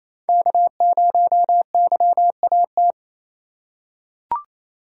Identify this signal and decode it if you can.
Morse code, heard as a keyed tone, about 28 wpm, 700 Hz. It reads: K0YAT
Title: USA Call Signs - 5 Characters - 4x Character Spacing - 28wpm